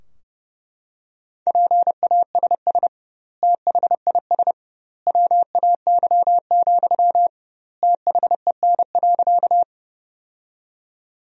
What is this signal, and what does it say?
Morse code, heard as a keyed tone, about 30 wpm, 700 Hz. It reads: PASS THIS WAY, THEN.
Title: 2x Word-Spacing - Sentences from Top 400 Words - 30wpm